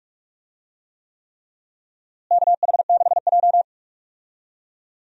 Morse code, heard as a keyed tone, about 43 wpm, 700 Hz. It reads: KH6J
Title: USA Call Signs - 4 Characters - Repeated 3x - 45wpm